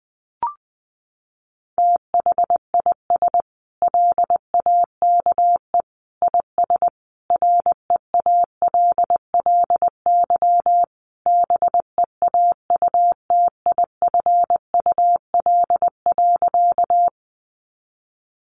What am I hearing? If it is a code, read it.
Morse code, heard as a keyed tone, about 20 wpm, 700 Hz. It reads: THIS LAKE IS REALLY BEAUTIFUL.
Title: Sentences from Top 1100 Words 20wpm